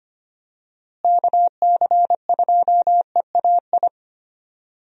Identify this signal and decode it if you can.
Morse code, heard as a keyed tone, about 25 wpm, 700 Hz. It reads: KC2EAI
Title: USA Call Signs - 6 Characters - Repeated 2x - 25wpm